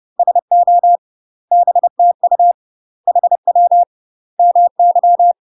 Morse code, heard as a keyed tone, about 30 wpm, 700 Hz. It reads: SO BTU HW MY
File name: UK MASTER QSOs TRAINING / 30WPM / UK MASTER QSOs 0001